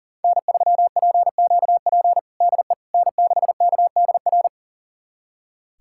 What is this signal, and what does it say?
Morse code, heard as a keyed tone, about 37 wpm, 700 Hz. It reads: N3PQP DE N6KDR
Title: USA Callsign DE Callsign - 5 Characters - 40wpm